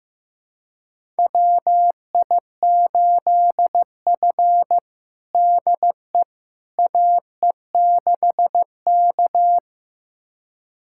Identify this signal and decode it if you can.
Morse code, heard as a keyed tone, about 15 wpm, 700 Hz. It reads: WI8F DE AE6K
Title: USA Callsign DE Callsign - 4 Characters - 15wpm